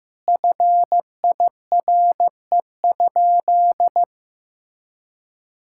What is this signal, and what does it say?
Morse code, heard as a keyed tone, about 15 wpm, 700 Hz. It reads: FIRE?
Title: Sentences from Top 1000 Words - Review Part 6 - 15wpm